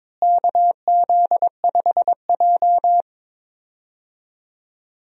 Morse code, heard as a keyed tone, about 22 wpm, 700 Hz. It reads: KZ5J
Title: USA Call Signs - 4 Characters - Repeated 2x - 22wpm